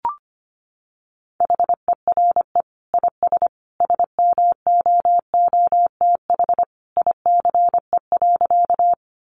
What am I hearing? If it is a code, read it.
Morse code, heard as a keyed tone, about 25 wpm, 700 Hz. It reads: HERE IS SMOOTH ICE.
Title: Sentences from Top 1400 Words 25wpm